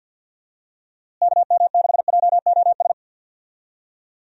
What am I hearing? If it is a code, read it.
Morse code, heard as a keyed tone, about 52 wpm, 700 Hz. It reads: KM6JOS
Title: USA Call Signs - 6 Characters - Repeated 3x - 50wpm